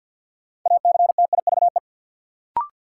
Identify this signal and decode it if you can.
Morse code, heard as a keyed tone, about 53 wpm, 700 Hz. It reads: ACTIVE
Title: Sentences from Top 900 Words - New Word Drill - 50wpm